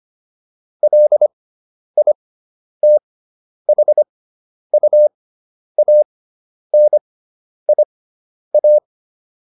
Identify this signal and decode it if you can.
Morse code, heard as a keyed tone, about 25 wpm, spaced as for 10 wpm, 600 Hz. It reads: LITHUANIA